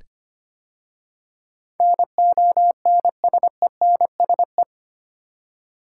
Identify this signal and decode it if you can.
Morse code, heard as a keyed tone, about 25 wpm, 700 Hz. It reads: NONSENSE
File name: 8 Letter Words - First 3 Letters Given - 25wpm